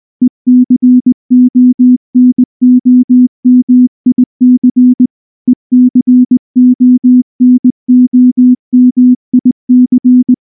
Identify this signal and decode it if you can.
Morse code, heard as a keyed tone, about 19 wpm, 250 Hz. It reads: ECONOMIC ECONOMIC
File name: morse / morse-training-20WPM